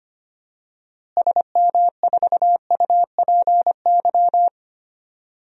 Morse code, heard as a keyed tone, about 25 wpm, 700 Hz. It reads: SM4UPY
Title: DXCC-100 Call Signs - 6 Characters - Repeated 2x - 25wpm